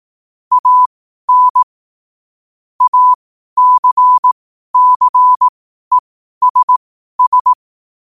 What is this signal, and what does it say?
Morse code, heard as a keyed tone, about 18 wpm, spaced as for 12 wpm, 1000 Hz. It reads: AN ACCESS
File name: qotdfiles / 12wpm